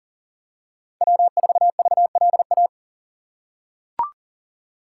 Morse code, heard as a keyed tone, about 40 wpm, 700 Hz. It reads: W4VLA
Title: USA Call Signs - 5 Characters - 6x Character Spacing - 40wpm